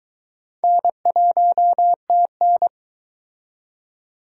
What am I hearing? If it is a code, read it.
Morse code, heard as a keyed tone, about 22 wpm, 700 Hz. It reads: N1TN